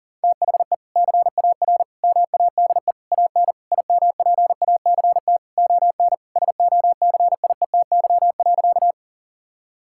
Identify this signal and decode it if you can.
Morse code, heard as a keyed tone, about 37 wpm, 700 Hz. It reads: THE CAR MADE AN IMPACT ON SOCIETY.
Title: Sentences from Top 1000 Words - Review Part 6 - 40wpm